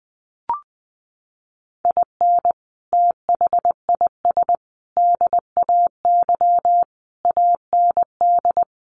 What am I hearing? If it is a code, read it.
Morse code, heard as a keyed tone, about 20 wpm, 700 Hz. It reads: IN THIS DAY AND
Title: Sets of 5 Words Encore 20wpm